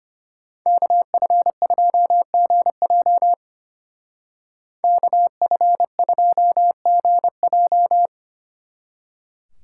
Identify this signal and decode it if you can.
Morse code, heard as a keyed tone, about 30 wpm, 700 Hz. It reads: KF2GJ KF2GJ
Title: Speed Racing - USA Call Signs 25wpm